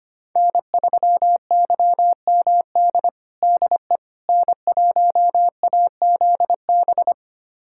Morse code, heard as a keyed tone, about 25 wpm, 700 Hz. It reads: N3YMD DE N1AZB
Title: USA Callsign DE Callsign - 5 Characters - 25wpm